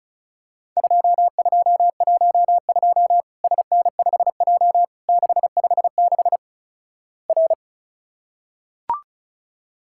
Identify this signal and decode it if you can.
Morse code, heard as a keyed tone, about 36 wpm, 700 Hz. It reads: 2212 SN5J 656 R
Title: Worked All Europe DX Contest - QTC Exchange - 35wpm